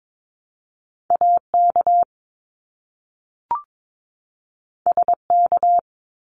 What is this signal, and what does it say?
Morse code, heard as a keyed tone, about 22 wpm, 700 Hz. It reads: AK SK
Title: ARRL Sections 22wpm